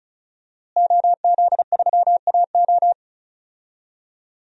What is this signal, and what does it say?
Morse code, heard as a keyed tone, about 34 wpm, 700 Hz. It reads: OZ3AO
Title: DXCC-100 Call Signs - 5 Characters - Repeated 2x - 35wpm